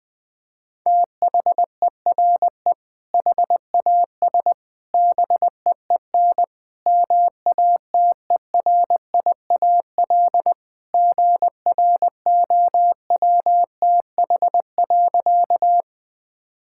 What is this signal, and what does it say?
Morse code, heard as a keyed tone, about 20 wpm, 700 Hz. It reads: THERE HAS BEEN MATERIAL GROWTH.
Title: Sentences from Top 1000 Words - Review Part 5 - 20wpm